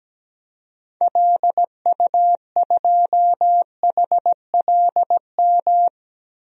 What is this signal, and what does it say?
Morse code, heard as a keyed tone, about 17 wpm, 700 Hz. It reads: LU2HLM
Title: DXCC-100 Call Signs - 6 Characters - Repeated 3x - 17wpm